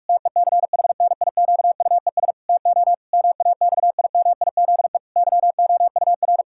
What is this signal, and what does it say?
Morse code, heard as a keyed tone, about 46 wpm, 700 Hz. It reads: TECHNIQUES TO MAXIMIZE YOUR
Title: Pangram Challenge v2 - 45wpm